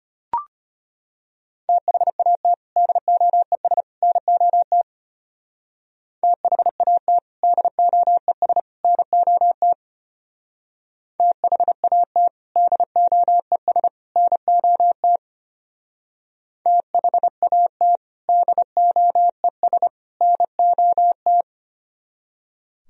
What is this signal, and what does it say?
Morse code, heard as a keyed tone, about 29 wpm, 700 Hz. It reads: THAT DOES NOT THAT DOES NOT THAT DOES NOT THAT DOES NOT
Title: Speed-Racing - Sets of 3 Words 25wpm